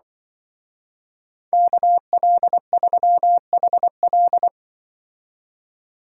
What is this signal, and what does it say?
Morse code, heard as a keyed tone, about 24 wpm, 700 Hz. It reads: KL3HL
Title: Speed Racing - USA 2x2 Call Signs 20wpm